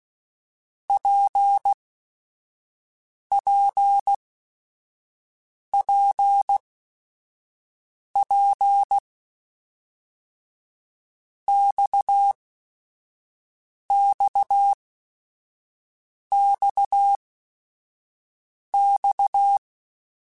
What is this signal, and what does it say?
Morse code, heard as a keyed tone, about 16 wpm, spaced as for 5 wpm, 800 Hz. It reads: PPPP XXXX
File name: Basic Sets / Basic Letter Groups Collective7 / RKPX@5